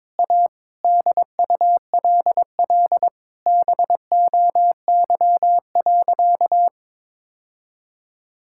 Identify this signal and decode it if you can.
Morse code, heard as a keyed tone, about 21 wpm, 700 Hz. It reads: A DULL BOY.
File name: practice / 22wpm / Sentences with 8-Word Phrases - 22wpm